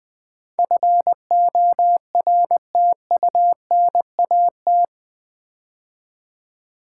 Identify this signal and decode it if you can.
Morse code, heard as a keyed tone, about 20 wpm, 700 Hz. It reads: FORTUNAT
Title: Build A Word - 4 Letters At A Time - 11-Letter Words - 20wpm